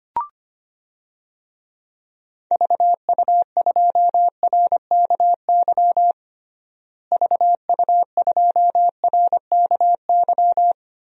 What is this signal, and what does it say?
Morse code, heard as a keyed tone, about 25 wpm, 700 Hz. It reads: VU2RKY VU2RKY
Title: DXCC-100 Call Signs - 6 Characters - Repeated 3x - 25wpm